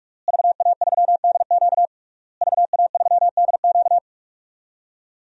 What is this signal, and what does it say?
Morse code, heard as a keyed tone, about 46 wpm, 700 Hz. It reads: VA3DQ VA3DQ
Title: DXCC-100 Call Signs - 5 Characters - Repeated 3x - 45wpm